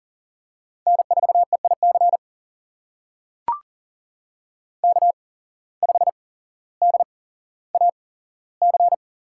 Morse code, heard as a keyed tone, about 39 wpm, spaced as for 17 wpm, 700 Hz. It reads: N4EIC K5DAC
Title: USA Call Signs - 5 Characters - 8x Character Spacing - 40wpm